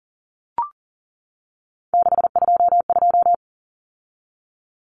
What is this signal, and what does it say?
Morse code, heard as a keyed tone, about 40 wpm, 700 Hz. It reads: 622
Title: Common Cut Numbers - 3-Digit 40wpm